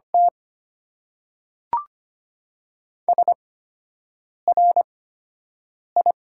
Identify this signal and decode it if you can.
Morse code, heard as a keyed tone, about 25 wpm, spaced as for 7 wpm, 700 Hz. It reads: T SRI
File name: Top 90 QSO Elements - 8x Character Spacing - 25wpm